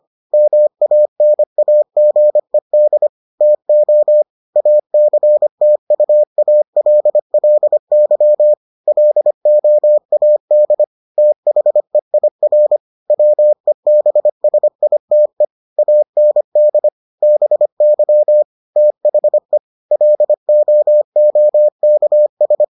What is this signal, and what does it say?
Morse code, heard as a keyed tone, about 25 wpm, 600 Hz. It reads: MANAGED TO ACTUALLY LOAD THEIR WEBSITE AND BY THE LOOKS